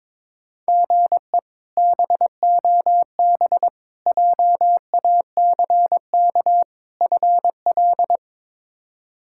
Morse code, heard as a keyed tone, about 22 wpm, 700 Hz. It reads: GE BOB JACK FL
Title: K1USN SST QSO Practice - 22wpm